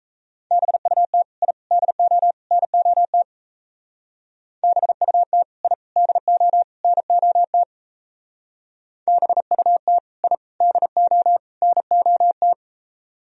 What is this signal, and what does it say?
Morse code, heard as a keyed tone, about 35 wpm, 700 Hz. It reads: BUT I DO NOT BUT I DO NOT BUT I DO NOT
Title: Speed-Racing - Sets of 4 Words 28wpm